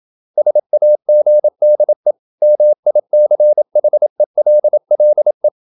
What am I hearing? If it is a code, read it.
Morse code, heard as a keyed tone, about 27 wpm, 600 Hz. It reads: SAGDE MICHELLE
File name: Ferie QRQ 27wpm